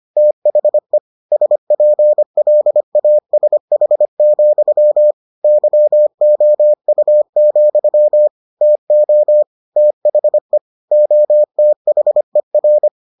Morse code, heard as a keyed tone, about 25 wpm, 600 Hz. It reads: THE SPLASH, YOU, TO THE OTHER